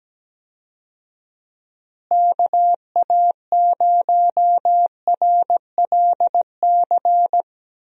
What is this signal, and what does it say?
Morse code, heard as a keyed tone, about 17 wpm, 700 Hz. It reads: KA0RLC
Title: USA Call Signs - 6 Characters - Repeated 3x - 17wpm